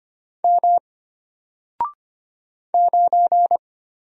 Morse code, heard as a keyed tone, about 25 wpm, 700 Hz. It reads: M 9
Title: Single Letter-Number Alt 25wpm